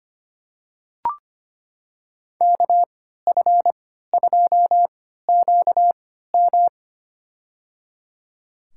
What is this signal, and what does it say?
Morse code, heard as a keyed tone, about 25 wpm, spaced as for 14 wpm, 700 Hz. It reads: KF2QM
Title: USA Call Signs - 5 Characters - 3x Character Spacing - 25wpm